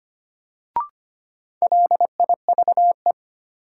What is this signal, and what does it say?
Morse code, heard as a keyed tone, about 25 wpm, 700 Hz. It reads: LIVE
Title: Sentences from Top 200 Words - New Word Drill - 25wpm